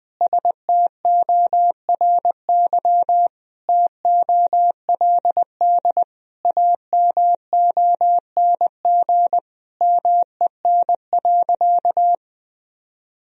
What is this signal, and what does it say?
Morse code, heard as a keyed tone, about 21 wpm, 700 Hz. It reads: STORY TOLD AMONG MEN.
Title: Sentences from Top 1000 Words - Review Part 3 - 20wpm